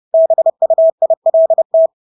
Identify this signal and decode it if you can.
Morse code, heard as a keyed tone, about 30 wpm, 650 Hz. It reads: BUILT
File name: Engelske ord 30-30 wpm